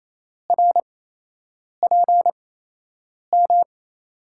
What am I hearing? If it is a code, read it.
Morse code, heard as a keyed tone, about 28 wpm, spaced as for 8 wpm, 700 Hz. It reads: RPM
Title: USA Call Signs - 5 Characters - 8x Character Spacing - 28wpm